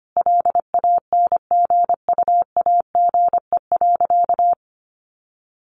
Morse code, heard as a keyed tone, about 25 wpm, 700 Hz. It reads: LANGUAGE.
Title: Sentences from Top 1300 Words 25wpm